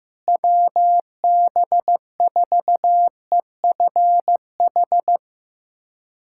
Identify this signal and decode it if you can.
Morse code, heard as a keyed tone, about 15 wpm, 700 Hz. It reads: WB4EFH
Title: USA Callsign DE Callsign 15wpm